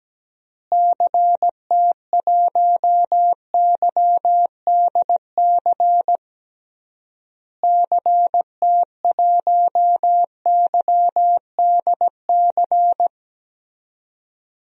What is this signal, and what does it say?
Morse code, heard as a keyed tone, about 17 wpm, 700 Hz. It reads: CT1YDC CT1YDC
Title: DXCC-100 Call Signs - 6 Characters - Repeated 3x - 17wpm